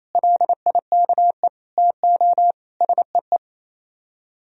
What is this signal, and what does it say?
Morse code, heard as a keyed tone, about 36 wpm, 700 Hz. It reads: LIKE TO SEE